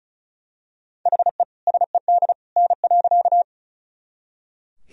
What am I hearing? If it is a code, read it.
Morse code, heard as a keyed tone, about 36 wpm, 700 Hz. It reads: HE SED N.
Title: Sentences from Top 100 Words and Abbreviations 35wpm